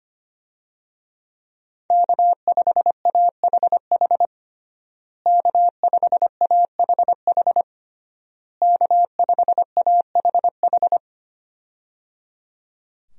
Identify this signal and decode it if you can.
Morse code, heard as a keyed tone, about 25 wpm, 700 Hz. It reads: K5AHH K5AHH K5AHH
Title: USA Call Signs - 5 Characters - Repeated 3x - 25wpm